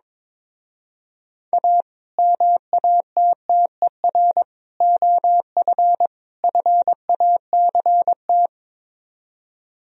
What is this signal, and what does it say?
Morse code, heard as a keyed tone, about 22 wpm, 700 Hz. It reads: A MATTER OF FACT
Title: Speed-Racing - Sets of 4 Words 22wpm